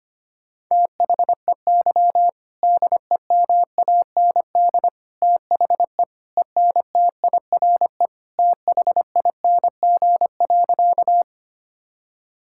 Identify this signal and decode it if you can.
Morse code, heard as a keyed tone, about 25 wpm, 700 Hz. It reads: THEY DEMAND THE ENTIRE THING.
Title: Sentences from Top 1000 Words - Review Part 5 - 25wpm